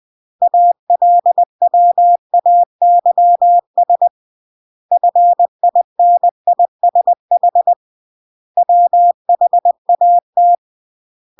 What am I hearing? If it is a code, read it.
Morse code, heard as a keyed tone, about 20 wpm, 700 Hz. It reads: ALWAYS FINISH WHAT